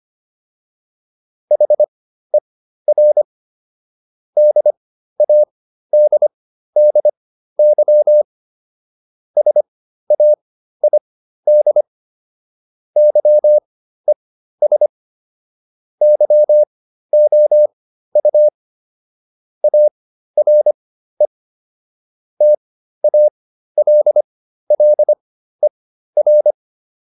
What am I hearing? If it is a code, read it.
Morse code, heard as a keyed tone, about 25 wpm, spaced as for 13 wpm, 600 Hz. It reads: HER DADDY SAID YES YOU ARE TALLER